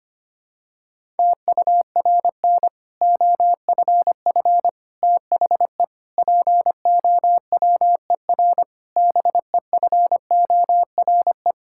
Morse code, heard as a keyed tone, about 25 wpm, 700 Hz. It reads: TURN OFF THE POWER BEFORE
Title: Sentences from Top 1000 Words - Review Part 6 - 25wpm